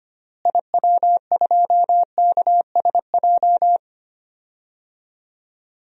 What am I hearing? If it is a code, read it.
Morse code, heard as a keyed tone, about 25 wpm, 700 Hz. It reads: IW2KSJ
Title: DXCC-100 Call Signs - 6 Characters - Repeated 2x - 25wpm